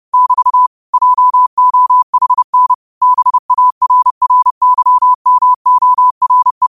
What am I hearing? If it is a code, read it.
Morse code, heard as a keyed tone, about 30 wpm, 1000 Hz. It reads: <BT> JOHN BARRYMORE